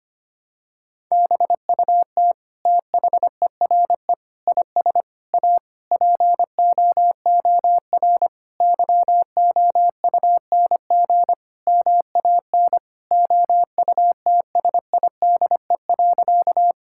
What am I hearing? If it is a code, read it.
Morse code, heard as a keyed tone, about 25 wpm, 700 Hz. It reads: BUT THERE IS A POOR YOUNG MAN OUTSIDE.
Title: Sentences from Top 1000 Words - Review Part 6 - 25wpm